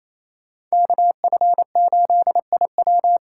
Morse code, heard as a keyed tone, about 35 wpm, 700 Hz. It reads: KF8IW